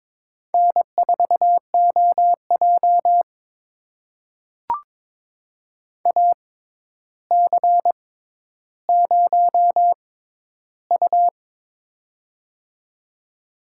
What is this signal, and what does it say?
Morse code, heard as a keyed tone, about 22 wpm, spaced as for 11 wpm, 700 Hz. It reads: N4OJ AC0U
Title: USA Call Signs - 4 Characters - 6x Character Spacing - 22wpm